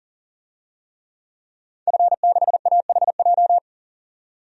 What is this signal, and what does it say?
Morse code, heard as a keyed tone, about 40 wpm, 700 Hz. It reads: F6AHJ
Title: DXCC-100 Call Signs - 5 Characters - 40wpm